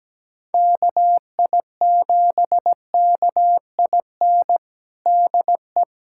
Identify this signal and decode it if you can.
Morse code, heard as a keyed tone, about 17 wpm, 700 Hz. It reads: KI7KIN DE
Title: USA Callsign DE Callsign 17wpm